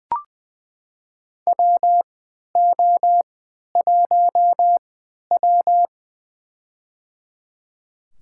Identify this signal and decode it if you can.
Morse code, heard as a keyed tone, about 21 wpm, spaced as for 11 wpm, 700 Hz. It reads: WO1W
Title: USA Call Signs - 4 Characters - 3x Character Spacing - 20wpm